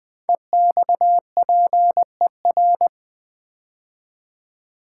Morse code, heard as a keyed tone, about 20 wpm, 700 Hz. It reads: EXPER
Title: Build A Word - 5 Letters At A Time - 12-Letter Words - 20wpm